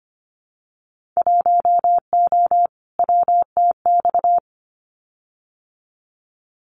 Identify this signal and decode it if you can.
Morse code, heard as a keyed tone, about 25 wpm, 700 Hz. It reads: 1O WTX
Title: Winter Field Day Exchanges Encore 25wpm